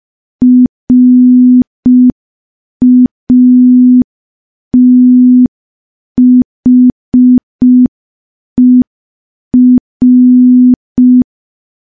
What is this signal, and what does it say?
Morse code, heard as a keyed tone, about 5 wpm, 250 Hz. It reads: RATHER
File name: morse / morse-training-05WPM